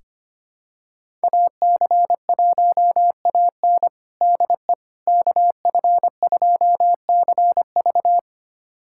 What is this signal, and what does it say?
Morse code, heard as a keyed tone, about 25 wpm, 700 Hz. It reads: AC1AN DE KF2CV
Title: USA Callsign DE Callsign - 5 Characters - 25wpm